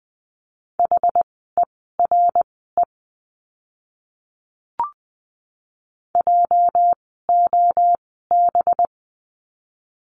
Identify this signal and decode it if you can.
Morse code, heard as a keyed tone, about 20 wpm, spaced as for 15 wpm, 700 Hz. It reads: HERE JOB
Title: 206 Lesson 33f - Letter j - Words 20wpm